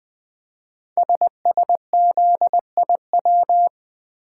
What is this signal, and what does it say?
Morse code, heard as a keyed tone, about 20 wpm, 700 Hz. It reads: SSZIW
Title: Code Groups - 5 Letters - 20wpm